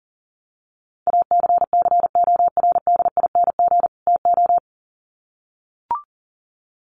Morse code, heard as a keyed tone, about 37 wpm, 700 Hz. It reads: ACCORDING TO
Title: Sets of 2 Words 40wpm